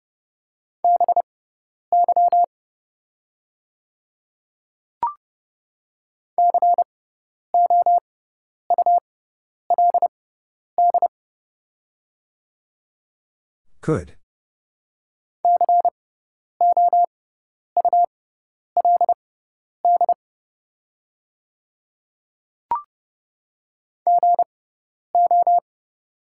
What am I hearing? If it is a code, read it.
Morse code, heard as a keyed tone, about 30 wpm, spaced as for 10 wpm, 690 Hz. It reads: BY COULD COULD GO
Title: 6x Character Spacing - Top 100 Words - 30wpm